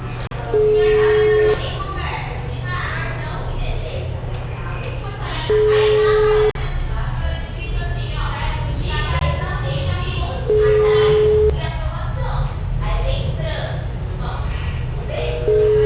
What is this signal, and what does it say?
Ringback tone (Europe, 425 Hz), with speech.
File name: out- -2011- - - .35359